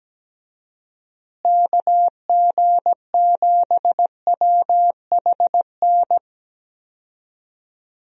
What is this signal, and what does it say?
Morse code, heard as a keyed tone, about 17 wpm, 700 Hz. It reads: KG7WHN
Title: USA Call Signs - 6 Characters - Repeated 2x - 17wpm